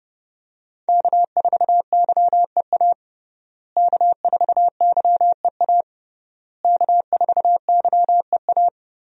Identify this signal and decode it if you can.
Morse code, heard as a keyed tone, about 30 wpm, 700 Hz. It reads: K4YEA K4YEA K4YEA
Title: USA Call Signs - 5 Characters - Repeated 3x - 30wpm